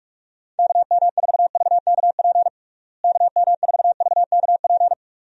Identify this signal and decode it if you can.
Morse code, heard as a keyed tone, about 48 wpm, 700 Hz. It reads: KM4VKP KM4VKP
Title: USA Call Signs - 6 Characters - Repeated 2x - 45wpm